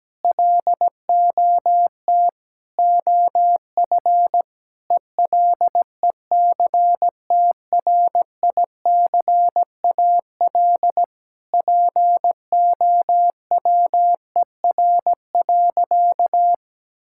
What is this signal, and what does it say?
Morse code, heard as a keyed tone, about 17 wpm, 700 Hz. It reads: LOT OF ELECTRICAL POWER.
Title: Sentences from Top 1000 Words - Review Part 6 - 17wpm